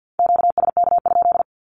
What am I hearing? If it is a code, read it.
Morse code, heard as a keyed tone, about 51 wpm, 700 Hz. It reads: QSK?
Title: Q-Codes Repeated 3x 50wpm